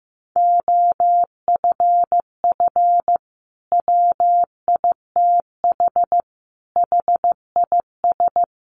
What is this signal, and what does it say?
Morse code, heard as a keyed tone, about 15 wpm, 700 Hz. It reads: OFF WITH HIS
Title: Sentences from Top 300 Words 15wpm